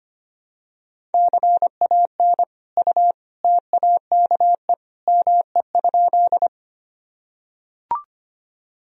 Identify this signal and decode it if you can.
Morse code, heard as a keyed tone, about 25 wpm, 700 Hz. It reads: CAN U TAKE ME?
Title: Sentences from Top 200 Words and Abbreviations 25wpm